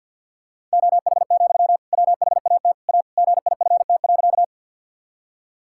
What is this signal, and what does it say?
Morse code, heard as a keyed tone, about 50 wpm, 700 Hz. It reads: OH, WHAT A GIFT.
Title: Sentences from Top 1000 Words - Review Part 6 - 50wpm